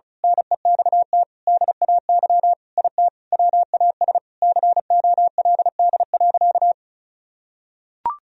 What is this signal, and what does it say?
Morse code, heard as a keyed tone, about 34 wpm, 700 Hz. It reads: NEXT DAY IT WAS COLD.
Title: Sentences from Top 1000 Words - Review Part 1 - 35wpm